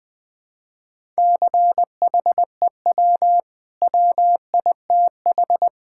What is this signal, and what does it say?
Morse code, heard as a keyed tone, about 20 wpm, 700 Hz. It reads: CHEW WITH
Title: Easy Sentences with Hints - 20wpm